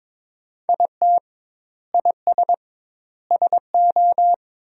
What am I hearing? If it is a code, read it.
Morse code, heard as a keyed tone, about 22 wpm, 700 Hz. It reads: IT IS SO
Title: 2x Word-Spacing - Sentences from Top 300 Words - 22wpm